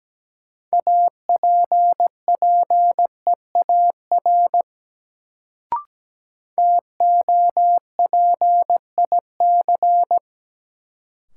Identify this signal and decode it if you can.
Morse code, heard as a keyed tone, about 17 wpm, 700 Hz. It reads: APPEAR TOPIC
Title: Sentences from Top 1200 Words - New Word Drill Alt - 17wpm